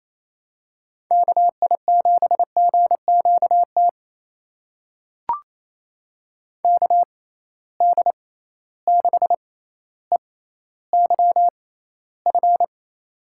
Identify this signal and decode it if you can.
Morse code, heard as a keyed tone, about 28 wpm, spaced as for 14 wpm, 700 Hz. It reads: KI7GQT KD6EYF
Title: USA Call Signs - 6 Characters - 6x Character Spacing - 28wpm